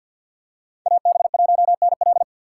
Morse code, heard as a keyed tone, about 53 wpm, 700 Hz. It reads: AB1NL
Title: USA Call Signs - 2x Character Spacing - 50wpm